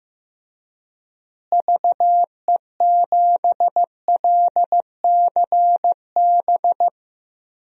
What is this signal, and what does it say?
Morse code, heard as a keyed tone, about 15 wpm, 700 Hz. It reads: VE7LCB
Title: DXCC-100 Call Signs - 6 Characters - Repeated 3x - 15wpm